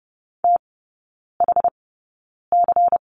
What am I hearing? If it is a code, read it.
Morse code, heard as a keyed tone, about 30 wpm, spaced as for 9 wpm, 690 Hz. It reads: THC
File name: Lesson 16e - Letter c - Three Characters 30wpm